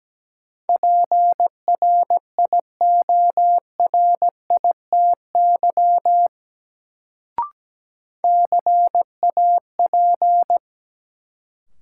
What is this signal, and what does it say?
Morse code, heard as a keyed tone, about 17 wpm, 700 Hz. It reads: PRIORITY CAP
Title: Sentences from Top 1600 Words - New Word Drill Alt - 17wpm